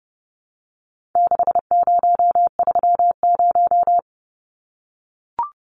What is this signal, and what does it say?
Morse code, heard as a keyed tone, about 28 wpm, 700 Hz. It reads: 6030
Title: Numbers - 4-Digit 30wpm